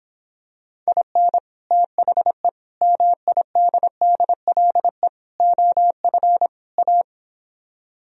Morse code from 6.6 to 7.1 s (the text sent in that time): A